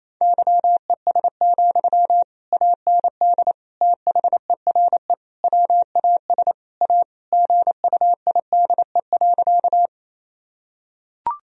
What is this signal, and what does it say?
Morse code, heard as a keyed tone, about 28 wpm, 700 Hz. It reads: YES, AND THERE WAS A GUIDE.
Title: Sentences from Top 1000 Words - Review Part 3 - 28wpm